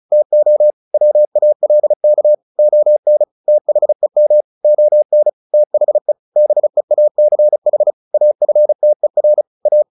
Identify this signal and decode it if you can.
Morse code, heard as a keyed tone, about 35 wpm, 600 Hz. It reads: TO WALK ON THEM ON THE BEACH AFTER A
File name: the old man and the sea / the old man and the sea 1 35wpm